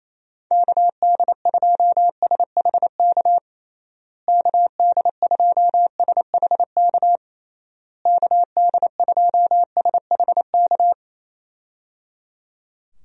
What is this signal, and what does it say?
Morse code, heard as a keyed tone, about 28 wpm, 700 Hz. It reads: KD2SHK KD2SHK KD2SHK
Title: USA Call Signs - 6 Characters - Repeated 3x - 28wpm